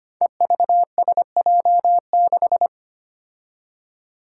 Morse code, heard as a keyed tone, about 25 wpm, 700 Hz. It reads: EVSJ6
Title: Five Challenging Letter-Number Combinations 25wpm